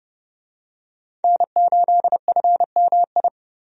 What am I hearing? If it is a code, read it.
Morse code, heard as a keyed tone, about 30 wpm, 700 Hz. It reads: N8FMI
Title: USA Call Signs - 5 Characters - 30wpm